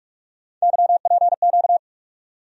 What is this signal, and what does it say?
Morse code, heard as a keyed tone, about 41 wpm, 700 Hz. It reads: YPQ
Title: USA Call Sign Postfixes - 3 Letters 45wpm